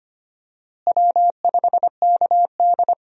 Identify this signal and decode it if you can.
Morse code, heard as a keyed tone, about 25 wpm, 700 Hz. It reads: W5KD
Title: USA Call Signs - 4 Characters - Repeated 3x - 25wpm